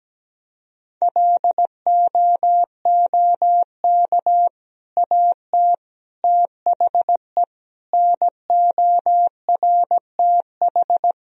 Morse code, heard as a keyed tone, about 17 wpm, 700 Hz. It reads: LOOK AT THE NORTH
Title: Sentences from Top 1000 Words - Review Part 2 - 17wpm